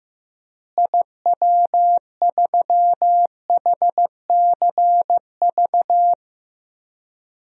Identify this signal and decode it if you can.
Morse code, heard as a keyed tone, about 15 wpm, 700 Hz. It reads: IW3HCV
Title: DXCC-100 Call Signs - 6 Characters - Repeated 3x - 15wpm